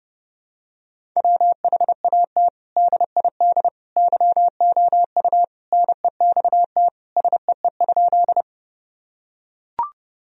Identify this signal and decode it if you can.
Morse code, heard as a keyed tone, about 32 wpm, 700 Hz. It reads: WHAT DID YOU NEXT SEE?
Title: Sentences from Top 1000 Words - Review Part 1 - 30wpm